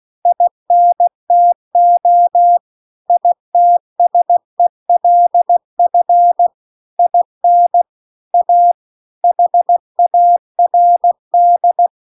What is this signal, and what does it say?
Morse code, heard as a keyed tone, about 16 wpm, 700 Hz. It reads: INTO ITSELF IN A HARD